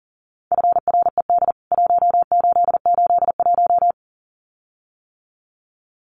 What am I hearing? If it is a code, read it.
Morse code, heard as a keyed tone, about 40 wpm, 700 Hz. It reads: FRED 1881
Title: CWT Exchange 40wpm